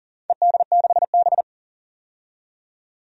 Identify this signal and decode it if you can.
Morse code, heard as a keyed tone, about 40 wpm, 700 Hz. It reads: ED6B
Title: DXCC-100 Call Signs - 4 Characters - 40wpm